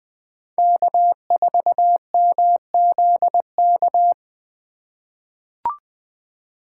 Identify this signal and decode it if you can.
Morse code, heard as a keyed tone, about 20 wpm, 700 Hz. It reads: K4MZK
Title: USA Call Signs - 5 Characters - 3x Character Spacing - 20wpm